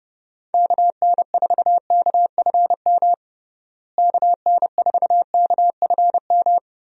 Morse code, heard as a keyed tone, about 29 wpm, 700 Hz. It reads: KN4KFM KN4KFM
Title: USA Call Signs - 6 Characters - Repeated 3x - 30wpm